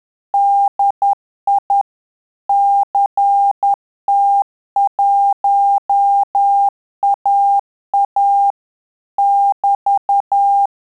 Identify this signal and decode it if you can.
Morse code, heard as a keyed tone, about 10 wpm, 800 Hz. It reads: DI CT1AA <BT>
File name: Telegraphy-group / QSO CW 10 PPM